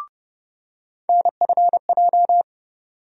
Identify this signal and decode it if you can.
Morse code, heard as a keyed tone, about 30 wpm, 700 Hz. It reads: NFJ
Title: USA Call Sign Postfixes - 3 Letters 30wpm